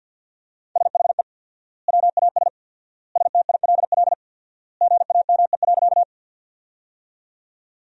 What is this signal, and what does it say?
Morse code, heard as a keyed tone, about 47 wpm, 700 Hz. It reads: SHE WAS STILL GAME.
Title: 4x Word-Spacing - Sentences from Top 500 Words - 50wpm